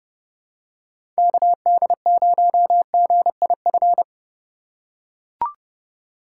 Morse code, heard as a keyed tone, about 30 wpm, 700 Hz. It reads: KD0GIF
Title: USA Call Signs - 6 Characters - 3x Character Spacing - 30wpm